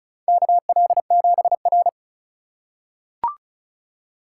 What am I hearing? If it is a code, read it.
Morse code, heard as a keyed tone, about 35 wpm, 700 Hz. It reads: KL7R
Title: USA Call Signs - 4 Characters - 4x Character Spacing - 35wpm